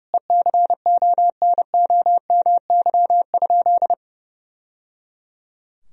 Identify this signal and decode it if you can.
Morse code, heard as a keyed tone, about 30 wpm, 700 Hz. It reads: ECONOMY?
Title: Sentences from Top 1000 Words - Review Part 3 - 30wpm